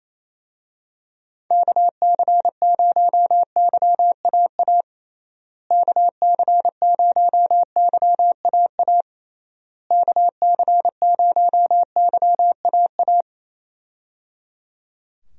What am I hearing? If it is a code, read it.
Morse code, heard as a keyed tone, about 27 wpm, 700 Hz. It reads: KC0YAA KC0YAA KC0YAA
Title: USA Call Signs - 6 Characters - Repeated 3x - 28wpm